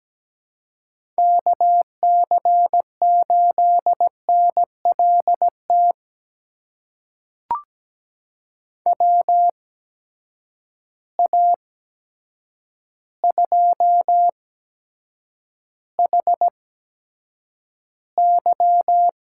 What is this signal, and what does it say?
Morse code, heard as a keyed tone, about 17 wpm, spaced as for 8 wpm, 700 Hz. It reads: KC8NLT WA2HY
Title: USA Call Signs - 8x Character Spacing - 17wpm